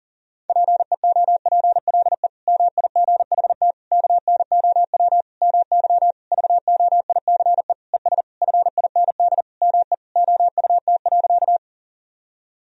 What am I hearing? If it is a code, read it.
Morse code, heard as a keyed tone, about 40 wpm, 700 Hz. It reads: PEOPLE MIGHT KNOW MY VOICE ES FIND ME OUT.
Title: Sentences from Top 500 Words and Abbreviations 40wpm